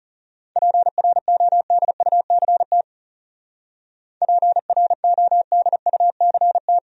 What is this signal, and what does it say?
Morse code, heard as a keyed tone, about 40 wpm, 700 Hz. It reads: PRODUCT PRODUCT